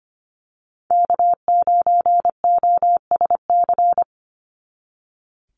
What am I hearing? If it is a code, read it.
Morse code, heard as a keyed tone, about 25 wpm, 700 Hz. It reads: K9OSC
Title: NA SOTA Chaser Callsigns 25wpm